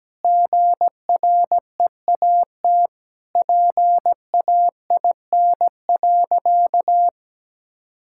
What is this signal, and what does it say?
Morse code, heard as a keyed tone, about 17 wpm, 700 Hz. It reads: GREAT PAIN.
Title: Sentences from Top 1600 Words 17wpm